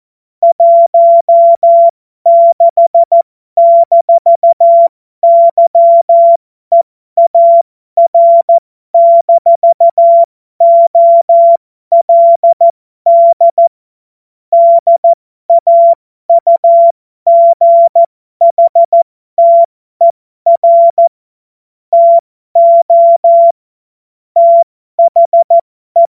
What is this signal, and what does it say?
Morse code, heard as a keyed tone, about 14 wpm, spaced as for 12 wpm, 680 Hz. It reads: 16-YEAR-OLD DAUGHTER TO THE
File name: Morsekurs / Witze